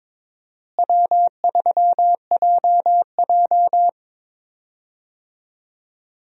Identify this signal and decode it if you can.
Morse code, heard as a keyed tone, about 23 wpm, 700 Hz. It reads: W3JJ
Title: USA Call Signs - 4 Characters - Repeated 3x - 22wpm